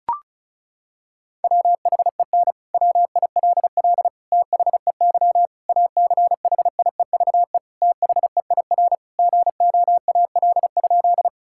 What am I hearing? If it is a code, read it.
Morse code, heard as a keyed tone, about 34 wpm, 700 Hz. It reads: WHEN WILL THEY ACHIEVE THEIR GOAL?
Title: Sentences from Top 1000 Words - Review Part 6 - 35wpm